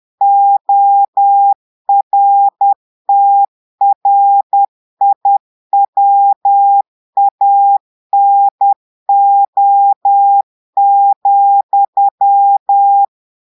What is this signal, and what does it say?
Morse code, heard as a keyed tone, about 10 wpm, 800 Hz. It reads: ORTRIWANO,
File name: Portugais - texte en clair 1 - vitesse - 10 mots minute